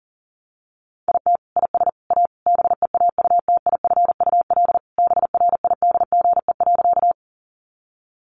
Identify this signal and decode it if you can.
Morse code, heard as a keyed tone, about 40 wpm, 700 Hz. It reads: IT IS A BEAUTIFUL BRIDGE.
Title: Sentences from Top 1200 Words 40wpm